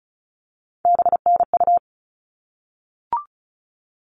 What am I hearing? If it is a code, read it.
Morse code, heard as a keyed tone, about 34 wpm, 700 Hz. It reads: BNU
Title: All Cut Numbers - 3-Digit 35wpm